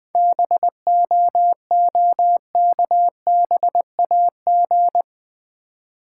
Morse code, heard as a keyed tone, about 20 wpm, 700 Hz. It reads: BOOKBAG
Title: Easy Words with Hints - 20wpm